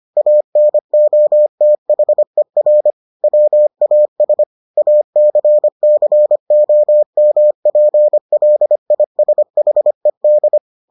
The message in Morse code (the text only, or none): ANOTHER WAS ACCOMPLISHED